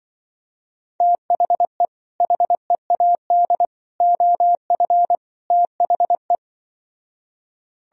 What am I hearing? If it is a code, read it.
Morse code, heard as a keyed tone, about 24 wpm, 700 Hz. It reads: THE HEAD OF THE
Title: Speed-Racing - Sets of 4 Words 20wpm